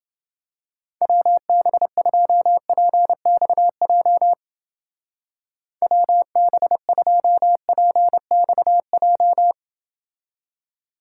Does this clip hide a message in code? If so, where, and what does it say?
Yes: Morse code from 1.0 to 9.5 s, heard as a keyed tone, about 29 wpm, 700 Hz. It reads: WB2PXJ WB2PXJ